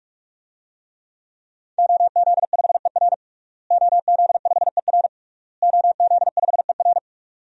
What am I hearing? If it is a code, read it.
Morse code, heard as a keyed tone, about 46 wpm, 700 Hz. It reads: OZ5ER OZ5ER OZ5ER
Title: DXCC-100 Call Signs - 5 Characters - Repeated 3x - 45wpm